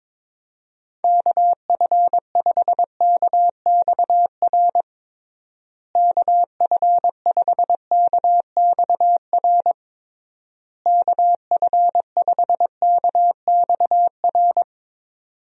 Morse code, heard as a keyed tone, about 22 wpm, 700 Hz. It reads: KF5KXR KF5KXR KF5KXR
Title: USA Call Signs - 6 Characters - Repeated 3x - 22wpm